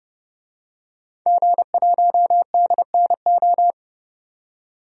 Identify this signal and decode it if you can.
Morse code, heard as a keyed tone, about 30 wpm, 700 Hz. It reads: G1DNO
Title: DXCC-100 Call Signs - 5 Characters - 30wpm